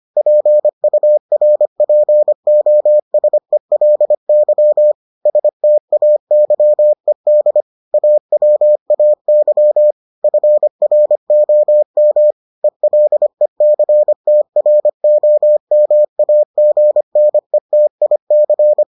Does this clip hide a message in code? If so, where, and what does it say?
Yes: Morse code from 0.2 to 18.8 s, heard as a keyed tone, about 25 wpm, 600 Hz. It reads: PURPOSELY STAYED AWAY FROM ELECTROMAGNETIC